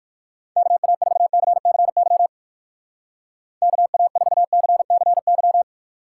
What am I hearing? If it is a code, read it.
Morse code, heard as a keyed tone, about 59 wpm, 700 Hz. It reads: KA4CCY KA4CCY